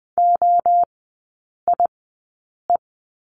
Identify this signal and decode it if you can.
Morse code, heard as a keyed tone, about 20 wpm, spaced as for 8 wpm, 700 Hz. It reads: OIE
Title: 047 Lesson 9e - Number 4 - Three Characters 20wpm